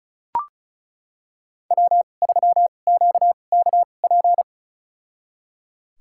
Morse code, heard as a keyed tone, about 36 wpm, spaced as for 25 wpm, 700 Hz. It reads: W3QKP
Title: USA Call Signs - 5 Characters - 2x Character Spacing - 35wpm